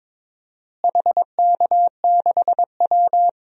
Morse code, heard as a keyed tone, about 21 wpm, 700 Hz. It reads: HK6W
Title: DXCC-100 Call Signs - 4 Characters - Repeated 2x - 22wpm